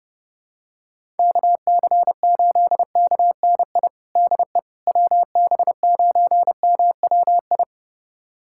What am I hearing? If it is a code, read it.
Morse code, heard as a keyed tone, about 32 wpm, 700 Hz. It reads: KC8KNI DE WB9MWI
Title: USA Callsign DE Callsign 30wpm